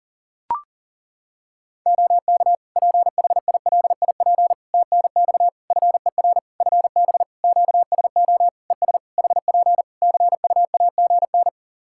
Morse code, heard as a keyed tone, about 40 wpm, 700 Hz. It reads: OK PHILIP TNX FER FB QSO ES HP CUAGN
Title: K1USN SST QSO Practice 40wpm